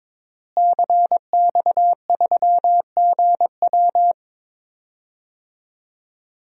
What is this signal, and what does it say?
Morse code, heard as a keyed tone, about 22 wpm, 700 Hz. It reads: CX3GW
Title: DXCC-100 Call Signs - 5 Characters - Repeated 2x - 22wpm